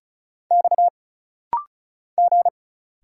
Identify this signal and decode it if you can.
Morse code, heard as a keyed tone, about 36 wpm, 700 Hz. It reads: X G
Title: Single Letters - Rapid-Fire 35wpm